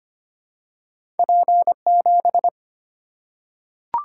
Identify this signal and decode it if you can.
Morse code, heard as a keyed tone, about 25 wpm, 700 Hz. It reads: P7
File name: Two Letter-Number Combinations - 25wpm